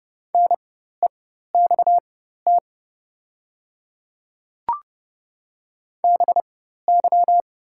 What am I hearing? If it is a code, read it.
Morse code, heard as a keyed tone, about 31 wpm, spaced as for 14 wpm, 690 Hz. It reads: NEXT BY
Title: 4x Character Spacing - Top 300 Words - 30wpm